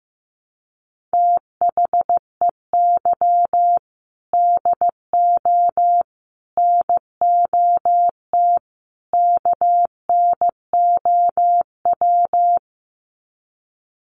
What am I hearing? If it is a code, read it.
Morse code, heard as a keyed tone, about 15 wpm, 700 Hz. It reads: THEY DO NOT KNOW
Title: Sets of 4 Words 15wpm